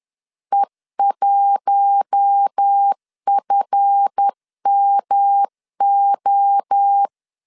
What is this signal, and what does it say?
Morse code, heard as a keyed tone, about 10 wpm, 800 Hz. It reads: E1FMO
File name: LEZIONI / 7 GF9Z / 7 GF9 10wpm 2747